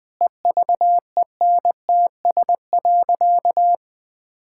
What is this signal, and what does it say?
Morse code, heard as a keyed tone, about 20 wpm, 700 Hz. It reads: EVENTS.
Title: Sentences with 8-Word Phrases - 20wpm